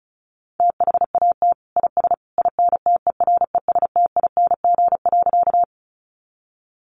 Morse code, heard as a keyed tone, about 34 wpm, 700 Hz. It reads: THAT IS INTERESTING.
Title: Sentences from Top 1000 Words 35wpm